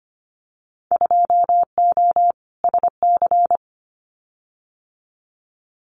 Morse code, heard as a keyed tone, about 25 wpm, 700 Hz. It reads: 2O SC
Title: Winter Field Day Exchanges Encore 25wpm